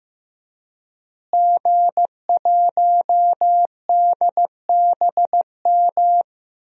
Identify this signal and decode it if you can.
Morse code, heard as a keyed tone, about 16 wpm, 700 Hz. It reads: G1DBM
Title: DXCC-100 Call Signs - 5 Characters - Repeated 3x - 15wpm